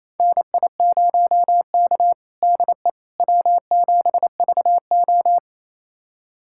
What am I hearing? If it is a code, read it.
Morse code, heard as a keyed tone, about 27 wpm, 690 Hz. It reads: NI0K DE W7VO
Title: USA Callsign DE Callsign - 4 Characters - 28wpm